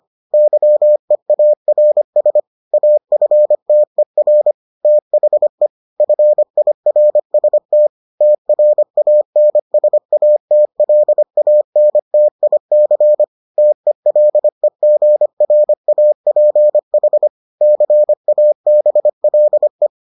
Morse code, heard as a keyed tone, about 25 wpm, 600 Hz. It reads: YEARS AFTER THE FIRST TRANSATLANTIC TELEGRAPH CABLE